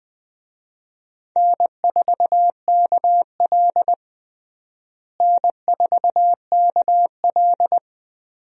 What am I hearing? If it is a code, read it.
Morse code, heard as a keyed tone, about 20 wpm, 700 Hz. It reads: N4KL N4KL
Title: USA Call Signs - 4 Characters - Repeated 2x - 20wpm